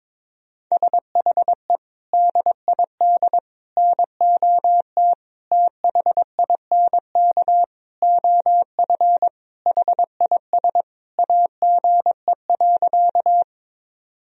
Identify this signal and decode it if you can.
Morse code, heard as a keyed tone, about 22 wpm, 700 Hz. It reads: SHE DID NOT THINK OF HIS AGE.
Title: Sentences from Top 1000 Words - Review Part 3 - 22wpm